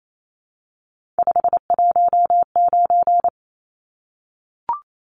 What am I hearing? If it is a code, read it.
Morse code, heard as a keyed tone, about 28 wpm, 700 Hz. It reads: 519
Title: Numbers - 3-Digit 28wpm